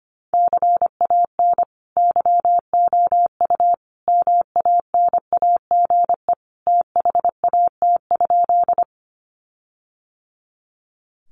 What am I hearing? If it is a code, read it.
Morse code, heard as a keyed tone, about 25 wpm, 700 Hz. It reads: CAN YOU MANAGE THAT?
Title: Sentences from Top 1400 Words 25wpm